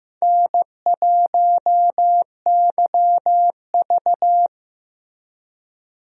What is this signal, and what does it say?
Morse code, heard as a keyed tone, about 15 wpm, 700 Hz. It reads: N1YV
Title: USA Call Signs - 4 Characters - Repeated 2x - 15wpm